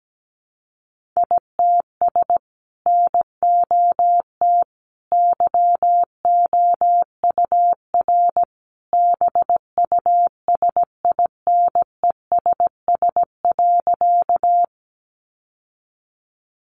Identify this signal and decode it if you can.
Morse code, heard as a keyed tone, about 17 wpm, 700 Hz. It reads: ITS NOT YOUR BUSINESS.
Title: Sentences from Top 600 Words 17wpm